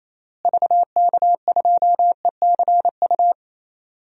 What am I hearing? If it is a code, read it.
Morse code, heard as a keyed tone, about 28 wpm, 700 Hz. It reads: VK2ECU
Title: DXCC-100 Call Signs - 6 Characters - Repeated 3x - 28wpm